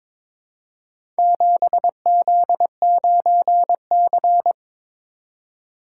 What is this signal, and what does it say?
Morse code, heard as a keyed tone, about 22 wpm, 700 Hz. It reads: 7Z9C
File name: practice / 22wpm / Four Letter-Number Combinations - 22wpm